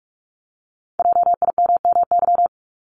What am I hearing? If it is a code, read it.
Morse code, heard as a keyed tone, about 46 wpm, 700 Hz. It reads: JIMMY
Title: Multi-letter Phonemes Repeated 3x 45wpm